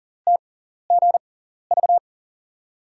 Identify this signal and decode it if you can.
Morse code, heard as a keyed tone, about 40 wpm, spaced as for 14 wpm, 700 Hz. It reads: TGV
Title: USA Call Signs - 5 Characters - 6x Character Spacing - 40wpm